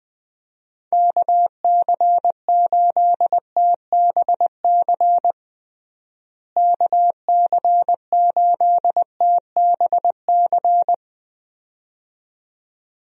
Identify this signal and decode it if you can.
Morse code, heard as a keyed tone, about 20 wpm, 700 Hz. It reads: KC8TBC KC8TBC
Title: USA Call Signs - 6 Characters - Repeated 2x - 20wpm